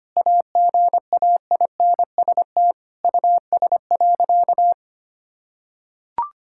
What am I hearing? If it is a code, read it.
Morse code, heard as a keyed tone, about 25 wpm, 700 Hz. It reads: AGAINST US.
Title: Sentences from Top 1000 Words - Review Part 6 - 25wpm